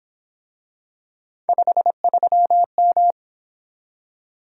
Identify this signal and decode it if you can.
Morse code, heard as a keyed tone, about 26 wpm, 700 Hz. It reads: 53M
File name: Speed-Racing - Three Letter-Number Combinations - 22wpm